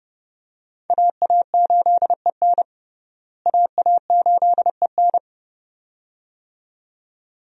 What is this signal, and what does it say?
Morse code, heard as a keyed tone, about 31 wpm, 700 Hz. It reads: AA8EN AA8EN
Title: USA Call Signs - 5 Characters - Repeated 2x - 30wpm